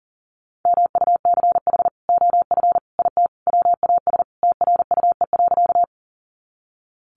Morse code, heard as a keyed tone, about 40 wpm, 700 Hz. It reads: MUCH OF IT WAS TRUE.
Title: Sentences from Top 400 Words 40wpm